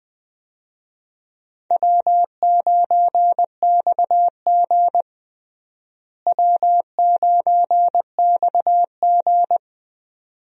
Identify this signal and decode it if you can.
Morse code, heard as a keyed tone, about 21 wpm, 700 Hz. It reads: W9XG W9XG
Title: USA Call Signs - 4 Characters - Repeated 2x - 20wpm